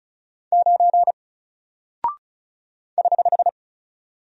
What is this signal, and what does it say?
Morse code, heard as a keyed tone, about 35 wpm, 700 Hz. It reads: 9 <HH>
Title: Single Character (Letter:Number:Punctuation:Prosign) 35wpm